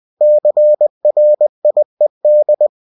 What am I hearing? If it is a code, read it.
Morse code, heard as a keyed tone, about 20 wpm, 600 Hz. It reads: CRIED